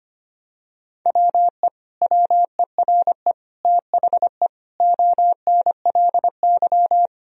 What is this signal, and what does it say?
Morse code, heard as a keyed tone, about 25 wpm, 700 Hz. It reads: WE WERE THE ONLY
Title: Sentences from Top 1000 Words - Review Part 1 - 25wpm